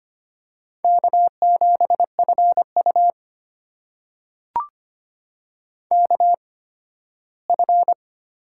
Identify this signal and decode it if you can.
Morse code, heard as a keyed tone, about 25 wpm, spaced as for 15 wpm, 700 Hz. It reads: K7FU KF
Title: USA Call Signs - 4 Characters - 8x Character Spacing - 25wpm